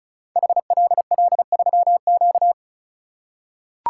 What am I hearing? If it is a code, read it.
Morse code, heard as a keyed tone, about 34 wpm, 690 Hz. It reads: HLL3Q
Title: Code Groups - 5 Letter:Numbers - 35wpm